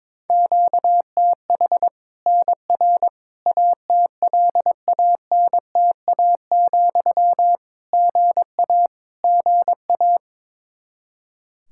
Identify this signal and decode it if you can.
Morse code, heard as a keyed tone, about 22 wpm, 700 Hz. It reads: QTH NR ATLANTA, GA GA
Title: K1USN SST QSO Practice 22wpm